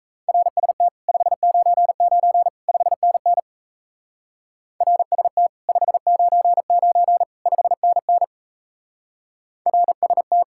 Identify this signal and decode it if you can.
Morse code, heard as a keyed tone, about 33 wpm, 700 Hz. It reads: RST 599 5NN RST 599 5NN RST
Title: Speed-Racing Common QSO Phrases Encore - 28wpm